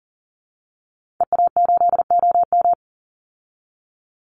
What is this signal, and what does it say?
Morse code, heard as a keyed tone, about 40 wpm, 700 Hz. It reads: EA8OM
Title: Call Signs Repeated 3x - 40wpm